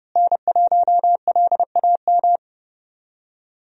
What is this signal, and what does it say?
Morse code, heard as a keyed tone, about 30 wpm, 700 Hz. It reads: N1LAM
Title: USA Call Signs - 5 Characters - Repeated 3x - 30wpm